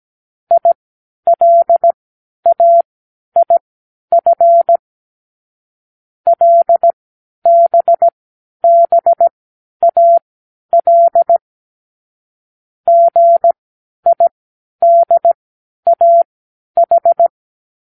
Morse code, heard as a keyed tone, about 17 wpm, spaced as for 10 wpm, 680 Hz. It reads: ILAIF LBBAL GIDAH
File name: k7qo / Lesson18(A-L)